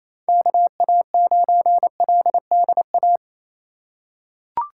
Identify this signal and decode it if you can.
Morse code, heard as a keyed tone, about 28 wpm, 700 Hz. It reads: KA9LDA
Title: USA Call Signs - 4x Character Spacing - 28wpm